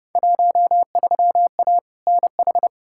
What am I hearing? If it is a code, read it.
Morse code, heard as a keyed tone, about 30 wpm, 700 Hz. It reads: 13A NH
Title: Field Day Exchanges Encore - 30wpm